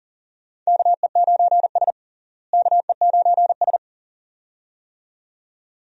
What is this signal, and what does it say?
Morse code, heard as a keyed tone, about 40 wpm, 700 Hz. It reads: KE9S KE9S
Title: USA Call Signs - 4 Characters - Repeated 2x - 40wpm